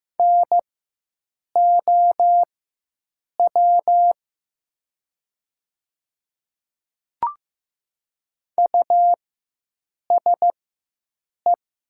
Morse code, heard as a keyed tone, about 15 wpm, spaced as for 7 wpm, 700 Hz. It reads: NOW USE
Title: 4x Character Spacing - Top 100 Words - 15wpm